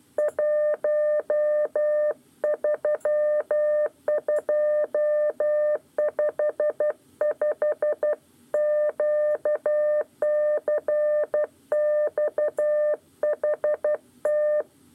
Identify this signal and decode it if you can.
Morse code, heard as a keyed tone, about 12 wpm, 600 Hz. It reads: 13255 QCXHT
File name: 8 lezione cw 1